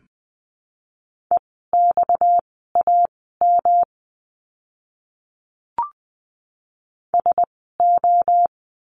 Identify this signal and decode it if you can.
Morse code, heard as a keyed tone, about 20 wpm, spaced as for 15 wpm, 700 Hz. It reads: EXAM SO
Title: 230 Lesson 37f - Letter x - Words 20wpm